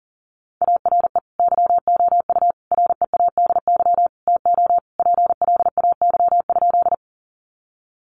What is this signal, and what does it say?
Morse code, heard as a keyed tone, about 40 wpm, 700 Hz. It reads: ARE YOU READY TO PLAY?
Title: Sentences from Top 400 Words 40wpm